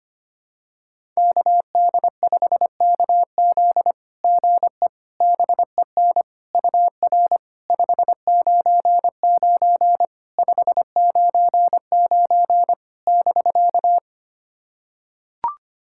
Morse code, heard as a keyed tone, about 25 wpm, 700 Hz. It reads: KD5KZ GE BEN UR 599 599 <BK>
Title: SOTA QSO Practice 25wpm